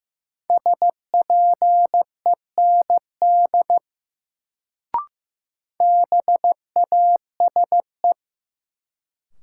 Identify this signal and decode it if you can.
Morse code, heard as a keyed tone, about 15 wpm, 700 Hz. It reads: SPEND BASE
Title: Sets of 5 Words - New Word Drill - 15wpm